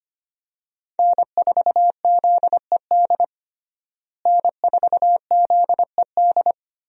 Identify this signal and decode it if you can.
Morse code, heard as a keyed tone, about 25 wpm, 700 Hz. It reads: N4ZED N4ZED
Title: USA Call Signs - 5 Characters - Repeated 3x - 25wpm